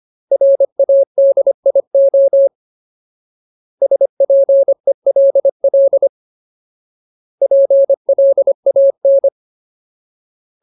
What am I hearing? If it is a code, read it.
Morse code, heard as a keyed tone, about 25 wpm, 550 Hz. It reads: RADIO SPELL PLAN
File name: Ultimate Word List / 25WPM Ultimate